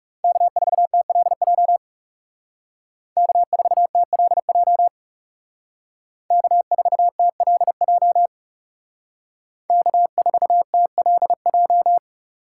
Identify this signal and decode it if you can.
Morse code, heard as a keyed tone, about 34 wpm, 700 Hz. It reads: K4TLJ K4TLJ K4TLJ K4TLJ
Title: Speed Racing - USA Call Signs 30wpm